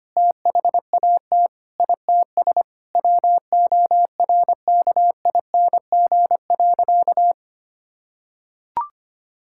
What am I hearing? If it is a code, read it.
Morse code, heard as a keyed tone, about 25 wpm, 700 Hz. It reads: THAT ITS WORKING.
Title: Sentences from Top 1000 Words - Review Part 6 - 25wpm